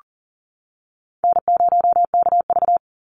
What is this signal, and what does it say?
Morse code, heard as a keyed tone, about 40 wpm, 700 Hz. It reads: N0KV
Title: Call Signs - 40wpm